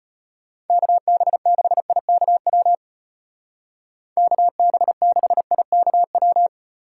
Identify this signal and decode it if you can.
Morse code, heard as a keyed tone, about 34 wpm, 700 Hz. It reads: KB6IKW KB6IKW